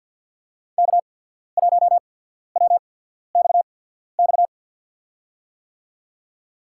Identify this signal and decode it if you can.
Morse code, heard as a keyed tone, about 53 wpm, spaced as for 14 wpm, 700 Hz. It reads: K1WXX
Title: USA Call Signs - 5 Characters - 8x Character Spacing - 50wpm